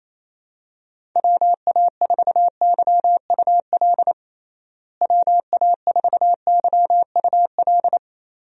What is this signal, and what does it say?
Morse code, heard as a keyed tone, about 28 wpm, 700 Hz. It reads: WA4YUL WA4YUL
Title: USA Call Signs - 6 Characters - Repeated 3x - 28wpm